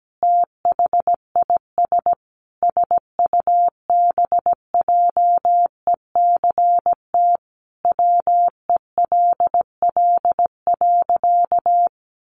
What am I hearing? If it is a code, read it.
Morse code, heard as a keyed tone, about 17 wpm, 700 Hz. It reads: THIS SUBJECT WELL.
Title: Sentences from Top 800 Words 17wpm